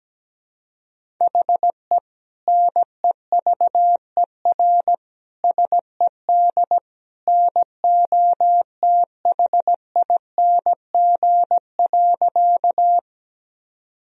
Morse code, heard as a keyed tone, about 17 wpm, 700 Hz. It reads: HE NEVER SED NOTHING.
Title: Sentences from Top 500 Words and Abbreviations 17wpm